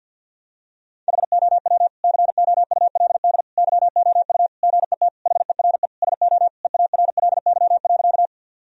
Morse code, heard as a keyed tone, about 50 wpm, 700 Hz. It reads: HOW COULD YOU GET HERE SO EARLY.
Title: Sentences from Top 1000 Words - Review Part 2 - 50wpm